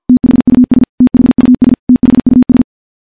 Morse code, heard as a keyed tone, about 51 wpm, 250 Hz. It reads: THUS THUS THUS
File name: morse / morse-training-50WPM